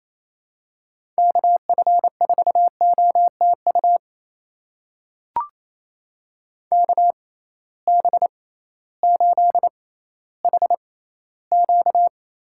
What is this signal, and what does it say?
Morse code, heard as a keyed tone, about 28 wpm, spaced as for 15 wpm, 700 Hz. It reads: KF4OTU KB8HQ
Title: USA Call Signs - 6 Characters - 6x Character Spacing - 28wpm